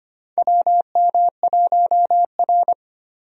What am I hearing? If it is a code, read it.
Morse code, heard as a keyed tone, about 25 wpm, 700 Hz. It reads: WM1R
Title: USA Call Signs - 4 Characters - 25wpm